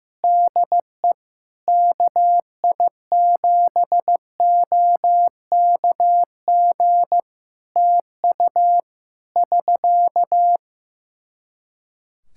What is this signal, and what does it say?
Morse code, heard as a keyed tone, about 15 wpm, 700 Hz. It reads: DE KI7OKG TU <SK>
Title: K1USN SST QSO Practice 15wpm